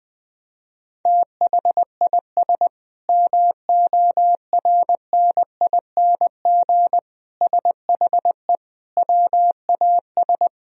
Morse code, heard as a keyed tone, about 20 wpm, 700 Hz. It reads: THIS MORNING SHE WAS
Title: Sentences from Top 1000 Words - Review Part 3 - 20wpm